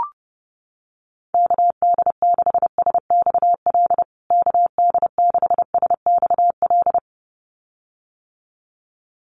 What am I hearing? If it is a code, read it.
Morse code, heard as a keyed tone, about 30 wpm, 700 Hz. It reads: KD6SXL KD6SXL
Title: USA Call Signs Repeated 2x 30wpm